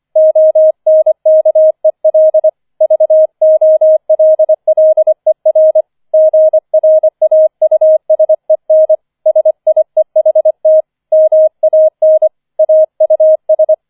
Morse code, heard as a keyed tone, about 25 wpm, 630 Hz. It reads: ONKEL VOLLER GRAUSEN SIEHT MAN AUS
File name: 005 Fünfter Streich 25wpm